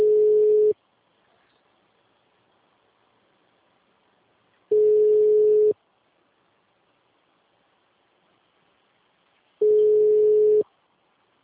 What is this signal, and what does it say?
Ringback tone (Europe, 425 Hz).